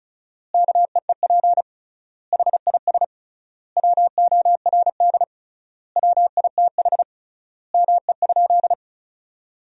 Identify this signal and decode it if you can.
Morse code, heard as a keyed tone, about 36 wpm, 700 Hz. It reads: KEEP HIS WORD WITH ME?
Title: 3x Word-Spacing - Sentences from Top 300 Words - 35wpm